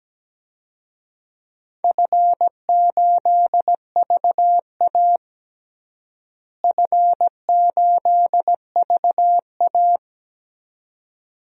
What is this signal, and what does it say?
Morse code, heard as a keyed tone, about 17 wpm, 700 Hz. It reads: F8VA F8VA
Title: DXCC-100 Call Signs - 4 Characters - Repeated 2x - 17wpm